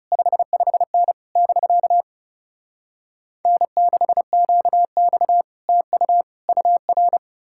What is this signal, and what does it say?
Morse code, heard as a keyed tone, about 40 wpm, 700 Hz. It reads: 55N <BK> N6QX TU UR